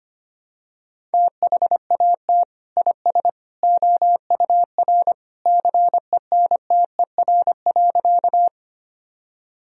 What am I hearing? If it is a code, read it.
Morse code, heard as a keyed tone, about 25 wpm, 700 Hz. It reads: THAT IS OUR CENTER.
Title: Sentences from Top 1000 Words - Review Part 2 - 25wpm